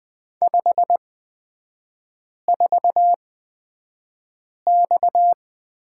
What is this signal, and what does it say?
Morse code, heard as a keyed tone, about 20 wpm, spaced as for 5 wpm, 700 Hz. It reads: 54X
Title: 255 Lesson 41e - Period - Three Characters 20wpm